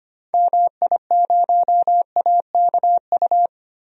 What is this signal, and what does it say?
Morse code, heard as a keyed tone, about 25 wpm, 700 Hz. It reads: MI0AKU
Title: DXCC-100 Call Signs - 6 Characters - 25wpm